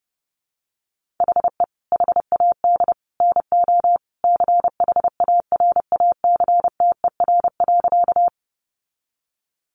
Morse code, heard as a keyed tone, about 30 wpm, 700 Hz. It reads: HE HAD NO CHARACTER.
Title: Sentences from Top 1200 Words 30wpm